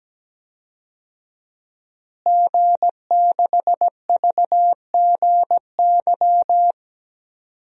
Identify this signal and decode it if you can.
Morse code, heard as a keyed tone, about 17 wpm, 700 Hz. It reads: G6VGY
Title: DXCC-100 Call Signs - 5 Characters - Repeated 2x - 17wpm